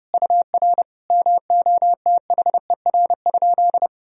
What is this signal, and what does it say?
Morse code, heard as a keyed tone, about 30 wpm, 700 Hz. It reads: UR MOTHER?
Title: Sentences from Top 200 Words and Abbreviations 30wpm